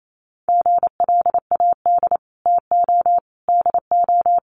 Morse code, heard as a keyed tone, about 28 wpm, 700 Hz. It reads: GLAD TO DO
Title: Sentences from Top 800 Words 28wpm